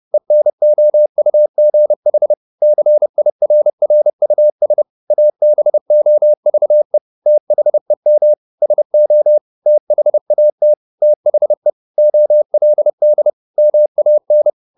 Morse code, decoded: ENOUGH CIRRUS ABOVE THEM SO THAT THE OLD MAN

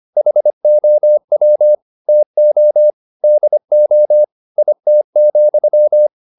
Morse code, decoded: HOW TO DO IT,

